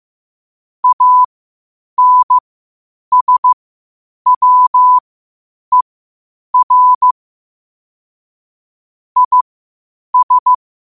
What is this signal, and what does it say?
Morse code, heard as a keyed tone, about 15 wpm, spaced as for 8 wpm, 1000 Hz. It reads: ANSWER IS
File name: qotdfiles / 8wpm